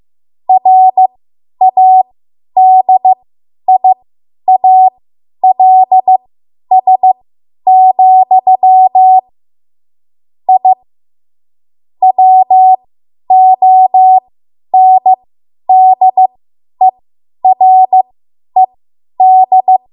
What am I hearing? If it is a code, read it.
Morse code, heard as a keyed tone, about 15 wpm, spaced as for 10 wpm, 750 Hz. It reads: RADIALS, I WONDERED
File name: Archive / 10 WPM / 10WPM